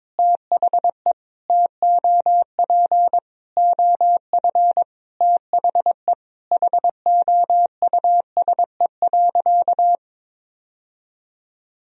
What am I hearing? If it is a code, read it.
Morse code, heard as a keyed tone, about 22 wpm, 700 Hz. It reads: THE TOP OF THE HOUSE.
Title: Sentences from Top 1000 Words - Review Part 2 - 22wpm